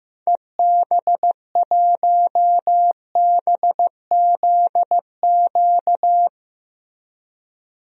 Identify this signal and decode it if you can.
Morse code, heard as a keyed tone, about 15 wpm, 700 Hz. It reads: EB1BZQ
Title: DXCC-100 Call Signs - 6 Characters - 15wpm